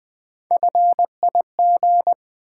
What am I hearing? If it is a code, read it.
Morse code, heard as a keyed tone, about 20 wpm, 700 Hz. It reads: FIG
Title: Build A Word - 1 Letter At A Time - 5-Letter Words - 20wpm